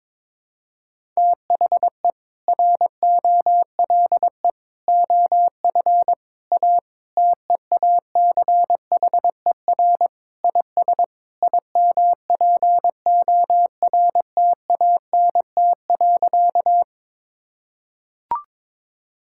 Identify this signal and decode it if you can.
Morse code, heard as a keyed tone, about 22 wpm, 700 Hz. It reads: THE ROLE OF A TEACHER IS IMPORTANT.
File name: practice / 22wpm / Sentences with 3-Word Phrases - 22wpm